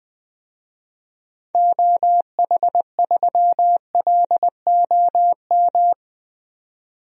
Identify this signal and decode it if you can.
Morse code, heard as a keyed tone, about 20 wpm, 700 Hz. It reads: OH3LOM
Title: DXCC-100 Call Signs - 6 Characters - 20wpm